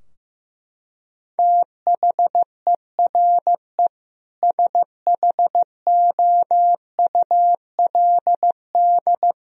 Morse code, heard as a keyed tone, about 15 wpm, 700 Hz. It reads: THERE SHOULD
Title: Sentences from Top 1000 Words - Review Part 3 - 15wpm